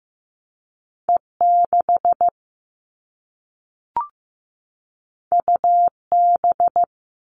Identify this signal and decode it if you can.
Morse code, heard as a keyed tone, about 15 wpm, 700 Hz. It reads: E6 UB
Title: All Cut Numbers - 2-Digit 15wpm